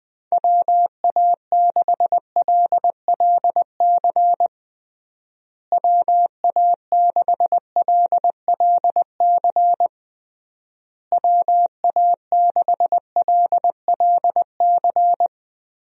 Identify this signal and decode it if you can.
Morse code, heard as a keyed tone, about 20 wpm, 700 Hz. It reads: WA6LLC WA6LLC WA6LLC
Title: USA Call Signs - 6 Characters - Repeated 3x - 20wpm